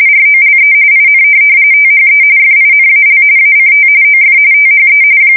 RTTY, about 45.45 baud. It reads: ROLE IN THE FINAL ACT OF THE
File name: fsk170 / USS-Missouri